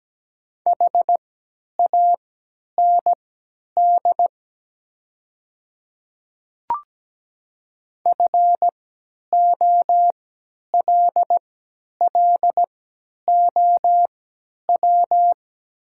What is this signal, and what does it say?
Morse code, heard as a keyed tone, about 17 wpm, spaced as for 10 wpm, 700 Hz. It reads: HAND FOLLOW
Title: 3x Character Spacing - Top 200 Words - 17wpm